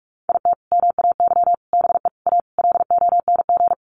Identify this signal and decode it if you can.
Morse code, heard as a keyed tone, about 44 wpm, 700 Hz. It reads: IT MAY BE A LONG
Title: Sentences from Top 1100 Words 45wpm